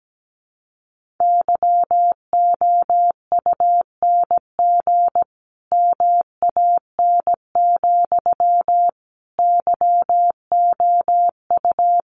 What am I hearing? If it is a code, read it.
Morse code, heard as a keyed tone, about 17 wpm, 700 Hz. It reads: YOUNG MAN, YOU
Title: Sentences from Top 400 Words 17wpm